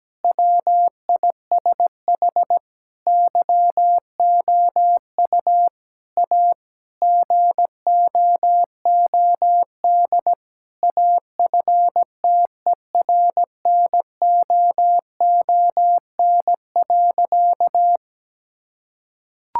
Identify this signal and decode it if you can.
Morse code, heard as a keyed tone, about 17 wpm, 700 Hz. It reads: WISH YOU A GOOD AFTERNOON.
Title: Sentences from Top 1600 Words 17wpm